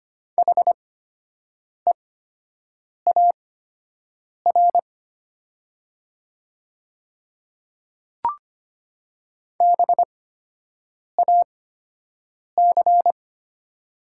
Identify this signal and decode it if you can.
Morse code, heard as a keyed tone, about 25 wpm, spaced as for 7 wpm, 700 Hz. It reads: HEAR BAC
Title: 8x Character Spacing - Top 500 Words - 25wpm